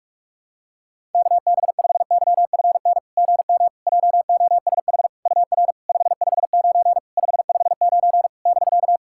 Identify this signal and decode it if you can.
Morse code, heard as a keyed tone, about 46 wpm, 700 Hz. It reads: KB5YFN GM JOSH UR 559 559 <BK>
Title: SOTA QSO Practice 45wpm